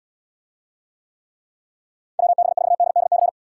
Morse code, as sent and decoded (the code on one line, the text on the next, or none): .-- -... ....- -.. -.. -.-.
WB4DDC